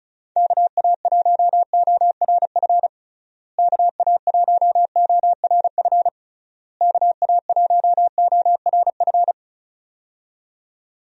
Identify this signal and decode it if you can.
Morse code, heard as a keyed tone, about 34 wpm, 700 Hz. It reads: KA1ORF KA1ORF KA1ORF
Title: USA Call Signs - 6 Characters - Repeated 3x - 35wpm